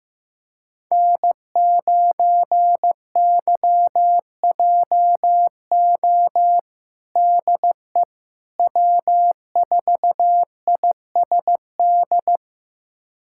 Morse code, heard as a keyed tone, about 16 wpm, 700 Hz. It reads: N9YJO DE W4ISD
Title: USA Callsign DE Callsign - 5 Characters - 15wpm